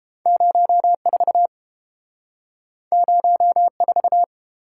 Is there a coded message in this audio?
Yes: Morse code from 0.3 to 4.2 s, heard as a keyed tone, about 30 wpm, 700 Hz. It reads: 04 04